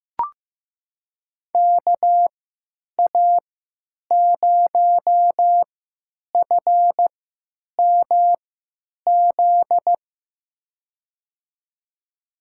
Morse code, heard as a keyed tone, about 15 wpm, spaced as for 9 wpm, 700 Hz. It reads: KA0FMZ
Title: USA Call Signs - 6 Characters - 3x Character Spacing - 15wpm